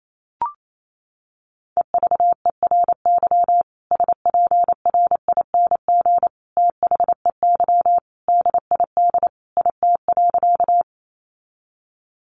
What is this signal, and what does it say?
Morse code, heard as a keyed tone, about 28 wpm, 700 Hz. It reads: EVERY SPRING THEY DID IT.
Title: Sentences from Top 1200 Words 28wpm